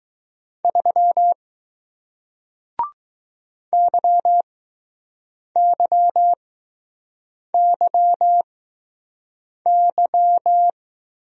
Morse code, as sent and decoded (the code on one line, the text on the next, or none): ...-- / -.-- -.-- -.-- -.--
3 YYYY